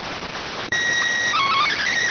SSTV (slow-scan television), PD 290.